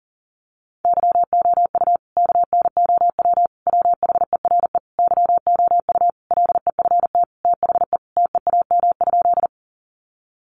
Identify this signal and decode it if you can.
Morse code, heard as a keyed tone, about 40 wpm, 700 Hz. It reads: YOU KNOW WHERE YOU LEFT THE TEAM?
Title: Sentences from Top 1500 Words 40wpm